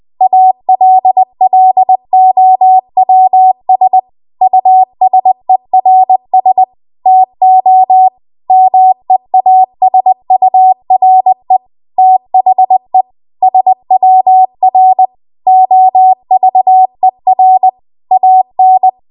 Morse code, heard as a keyed tone, about 20 wpm, 750 Hz. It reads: ALLOWS USERS TO MEASURE THE SWR OVER AN
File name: Archive / 20 WPM / 20WPM